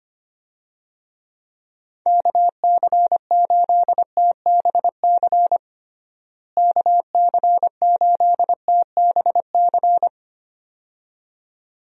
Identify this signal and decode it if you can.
Morse code, heard as a keyed tone, about 25 wpm, 700 Hz. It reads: KC8TBC KC8TBC
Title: USA Call Signs - 6 Characters - Repeated 2x - 25wpm